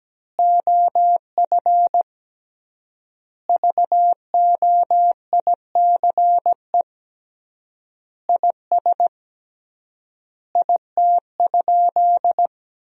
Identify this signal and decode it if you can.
Morse code, heard as a keyed tone, about 17 wpm, 700 Hz. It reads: OF VOICE IS IT?
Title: 3x Word-Spacing - Sentences from Top 500 Words - 17wpm